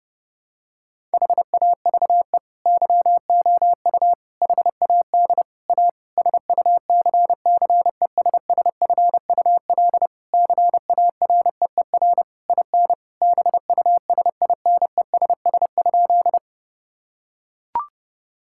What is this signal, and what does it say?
Morse code, heard as a keyed tone, about 30 wpm, 700 Hz. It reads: HAVE YOU HAD A SUCCESSFUL CAREER IN BUSINESS?
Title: Sentences from Top 1000 Words - Review Part 6 - 30wpm